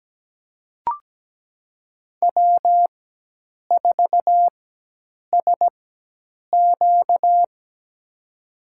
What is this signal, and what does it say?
Morse code, heard as a keyed tone, about 17 wpm, spaced as for 8 wpm, 700 Hz. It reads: W4SQ
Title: USA Call Signs - 4 Characters - 4x Character Spacing - 17wpm